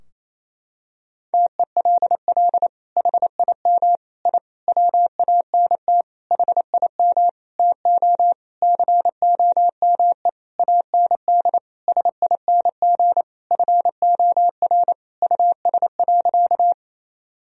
Morse code, heard as a keyed tone, about 28 wpm, 700 Hz. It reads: TELL HIM I WANT HIM TO COME AND SING FOR US.
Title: Sentences from Top 1000 Words - Review Part 2 - 28wpm